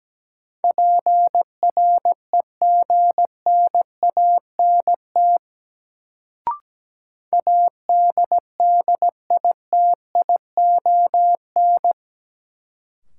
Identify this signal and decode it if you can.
Morse code, heard as a keyed tone, about 17 wpm, 700 Hz. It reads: PREGNANT ADDITION
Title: Sentences from Top 1000 Words - Review Part 6 - New Word Drill - 17wpm